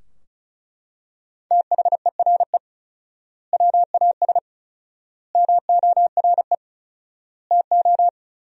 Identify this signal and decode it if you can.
Morse code, heard as a keyed tone, about 36 wpm, 700 Hz. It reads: THERE WAS MORE TO
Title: 4x Word-Spacing - Sentences from Top 500 Words - 35wpm